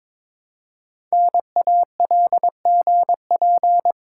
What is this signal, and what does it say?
Morse code, heard as a keyed tone, about 22 wpm, 700 Hz. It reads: NALGP
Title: Code Groups - 5 Letters - 22wpm